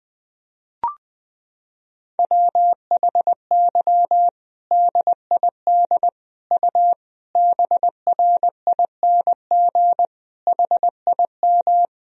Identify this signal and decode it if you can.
Morse code, heard as a keyed tone, about 20 wpm, 700 Hz. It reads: WHY DID U BRING HIM
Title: Sentences from Top 500 Words and Abbreviations 20wpm